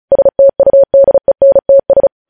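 Morse code, heard as a keyed tone, about 36 wpm, 550 Hz. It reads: STUDENTS
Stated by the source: Morse news headlines practice at 35 words per minute.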